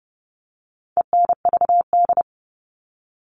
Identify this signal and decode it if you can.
Morse code, heard as a keyed tone, about 30 wpm, 700 Hz. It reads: ENVD
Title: All Cut Numbers - 4-Digit 30wpm